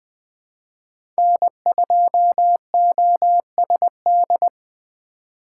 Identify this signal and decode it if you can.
Morse code, heard as a keyed tone, about 21 wpm, 700 Hz. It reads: N2OSD
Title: USA Call Signs - 2x Character Spacing - 20wpm